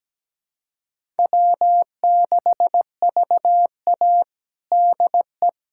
Morse code, heard as a keyed tone, about 17 wpm, 700 Hz. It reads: W6VA DE
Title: USA Callsign DE Callsign - 4 Characters - 17wpm